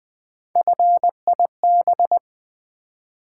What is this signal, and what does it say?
Morse code, heard as a keyed tone, about 20 wpm, 700 Hz. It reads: FIB
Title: Build A Word - 1 Letter At A Time - 5-Letter Words - 20wpm